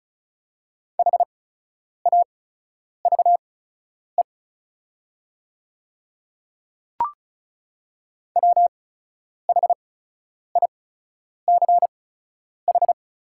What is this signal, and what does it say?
Morse code, heard as a keyed tone, about 36 wpm, spaced as for 10 wpm, 700 Hz. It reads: HAVE WHICH
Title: 8x Character Spacing - Top 100 Words - 35wpm